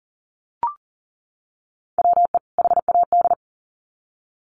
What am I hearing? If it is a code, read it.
Morse code, heard as a keyed tone, about 40 wpm, 700 Hz. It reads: WE HAD
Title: Sets of 2 Words 40wpm